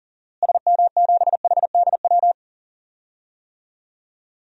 Morse code, heard as a keyed tone, about 40 wpm, 700 Hz. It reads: SM7HDW
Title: DXCC-100 Call Signs - 6 Characters - Repeated 2x - 40wpm